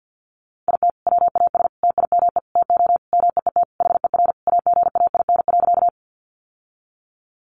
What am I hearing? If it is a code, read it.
Morse code, heard as a keyed tone, about 50 wpm, 700 Hz. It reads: IT WAS TIME TO MEET HER AGAIN.
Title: Sentences from Top 1100 Words 50wpm